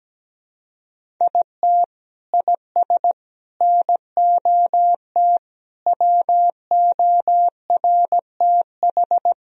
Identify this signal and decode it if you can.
Morse code, heard as a keyed tone, about 17 wpm, 700 Hz. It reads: IT IS NOT WORTH
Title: Sentences from Top 1000 Words - Review Part 1 - 17wpm